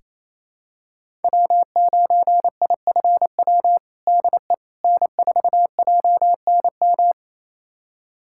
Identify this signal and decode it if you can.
Morse code, heard as a keyed tone, about 28 wpm, 700 Hz. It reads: W9IFW DE N4JNM
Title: USA Callsign DE Callsign - 5 Characters - 28wpm